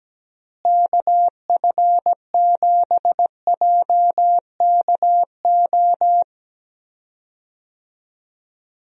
Morse code, heard as a keyed tone, about 17 wpm, 700 Hz. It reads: KF7JKO
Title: USA Call Signs - 6 Characters - Repeated 2x - 17wpm